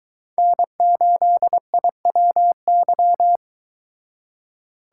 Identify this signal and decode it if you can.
Morse code, heard as a keyed tone, about 22 wpm, 700 Hz. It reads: N8IWY